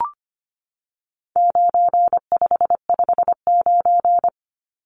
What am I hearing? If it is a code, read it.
Morse code, heard as a keyed tone, about 25 wpm, 700 Hz. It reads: 9559
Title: Numbers - 4-Digit 25wpm